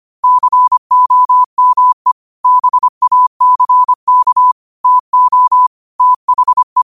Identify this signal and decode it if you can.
Morse code, heard as a keyed tone, about 25 wpm, 1000 Hz. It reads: COME BACK TO THE
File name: qotdfiles / 25wpm